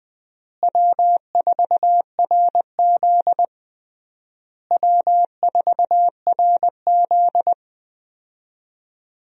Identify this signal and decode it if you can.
Morse code, heard as a keyed tone, about 20 wpm, 700 Hz. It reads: W4RZ W4RZ
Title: USA Call Signs - 4 Characters - Repeated 3x - 20wpm